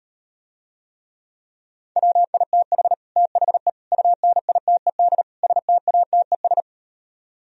Morse code, heard as a keyed tone, about 38 wpm, 700 Hz. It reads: WITH THE UNITED STATES